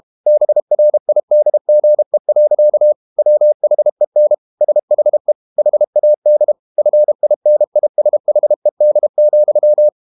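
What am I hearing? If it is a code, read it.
Morse code, heard as a keyed tone, about 32 wpm, 600 Hz. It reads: BRIDGE. WHEN SHE HAD FINISHED,
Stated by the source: Morse code practice at 32 wpm from CW med Gnister, reading Ken Follett's novel Never in English.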